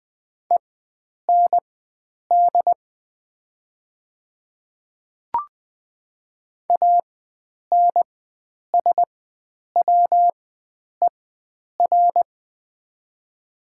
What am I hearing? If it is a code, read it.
Morse code, heard as a keyed tone, about 20 wpm, spaced as for 9 wpm, 700 Hz. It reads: END ANSWER
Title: 4x Character Spacing - Top 500 Words - 20wpm